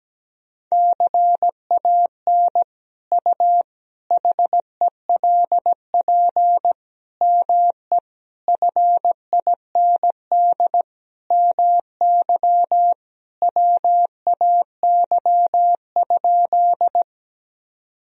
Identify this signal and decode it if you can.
Morse code, heard as a keyed tone, about 17 wpm, 700 Hz. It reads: CAN U HELP ME FIND MY WAY?
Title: Sentences from Top 200 Words and Abbreviations 17wpm